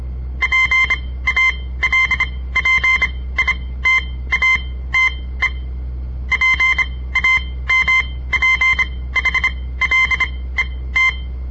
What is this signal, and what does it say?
Morse code, heard as a keyed tone, about 26 wpm, spaced as for 17 wpm, 2100 Hz. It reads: PALPITATE PAMPHLET
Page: On the air Morse code practice from the Southern Montana Amatuer Radio Association; all code, no read back